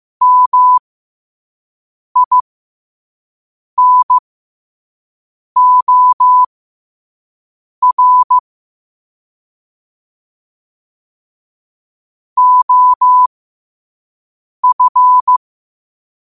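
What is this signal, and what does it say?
Morse code, heard as a keyed tone, about 15 wpm, spaced as for 5 wpm, 1000 Hz. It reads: MINOR OF
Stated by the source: Morse code sent at 5 words per minute.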